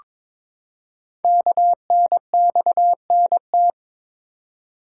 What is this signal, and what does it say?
Morse code, heard as a keyed tone, about 22 wpm, 700 Hz. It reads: KNXNT
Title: Code Groups - 5 Letters - 22wpm